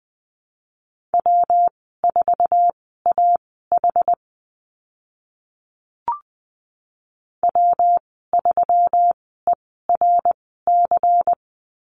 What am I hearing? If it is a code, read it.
Morse code, heard as a keyed tone, about 20 wpm, spaced as for 14 wpm, 700 Hz. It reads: W4AH W3ERC
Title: 122 Lesson 20g - Number 3 - Call Signs 20wpm